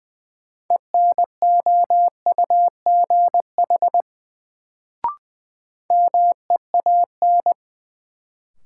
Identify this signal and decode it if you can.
Morse code, heard as a keyed tone, about 21 wpm, 700 Hz. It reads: ENOUGH MEAN
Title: Sets of 2 Words - New Word Drill - 20wpm